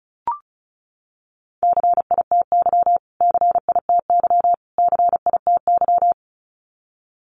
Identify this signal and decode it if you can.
Morse code, heard as a keyed tone, about 34 wpm, 700 Hz. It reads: CITY CITY CITY
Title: Top 300 Words Repeated 35wpm